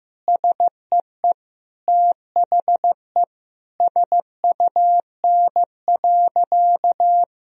Morse code, heard as a keyed tone, about 15 wpm, 700 Hz. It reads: SEE THE SUN.
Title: Sentences from Top 1000 Words - Review Part 1 - 15wpm